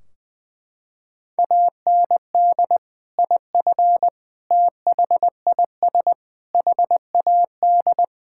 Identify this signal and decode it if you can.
Morse code, heard as a keyed tone, about 20 wpm, 700 Hz. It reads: AND IF THIS HAD
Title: Sentences from Top 1600 Words 20wpm